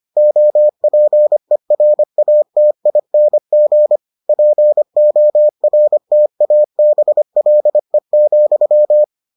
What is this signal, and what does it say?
Morse code, heard as a keyed tone, about 25 wpm, 600 Hz. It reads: OPERATING PORTABLE,